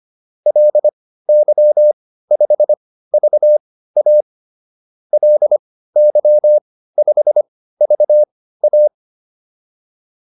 Morse code, heard as a keyed tone, about 25 wpm, spaced as for 15 wpm, 600 Hz. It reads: LY5VA LY5VA